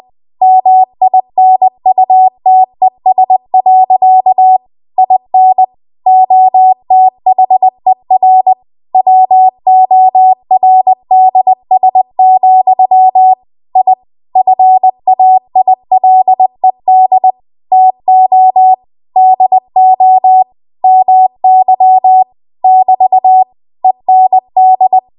Morse code, heard as a keyed tone, about 20 wpm, 750 Hz. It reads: MINUTES. IN OTHER WORDS, I FAILED TO DO MY <BT> END
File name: Archive / 20 WPM / 20WPM